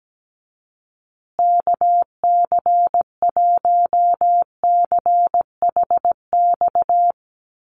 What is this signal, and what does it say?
Morse code, heard as a keyed tone, about 17 wpm, 700 Hz. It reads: KC1CHX
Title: USA Call Signs Repeated 3x 17wpm